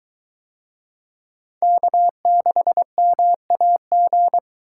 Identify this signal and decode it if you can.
Morse code, heard as a keyed tone, about 23 wpm, 700 Hz. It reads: K6MAG